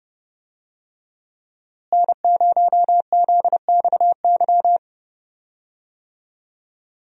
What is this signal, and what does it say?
Morse code, heard as a keyed tone, about 30 wpm, 700 Hz. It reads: N0ZXY
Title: USA Call Signs - 5 Characters - Repeated 3x - 30wpm